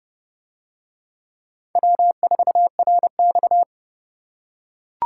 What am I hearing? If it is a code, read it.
Morse code, heard as a keyed tone, about 29 wpm, 700 Hz. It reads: W4RX